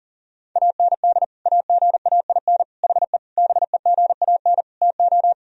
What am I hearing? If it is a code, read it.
Morse code, heard as a keyed tone, about 40 wpm, 700 Hz. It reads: AND AGAIN HE BEGAN TO
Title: Sentences from Top 1000 Words - Review Part 2 - 40wpm